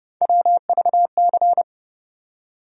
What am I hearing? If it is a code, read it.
Morse code, heard as a keyed tone, about 30 wpm, 700 Hz. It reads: WVC
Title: USA Call Sign Postfixes - 3 Letters 30wpm